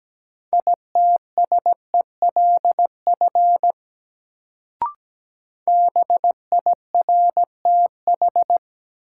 Morse code, heard as a keyed tone, about 17 wpm, 700 Hz. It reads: ITSELF BIRTH
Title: Sentences from Top 800 Words - New Word Drill - 17wpm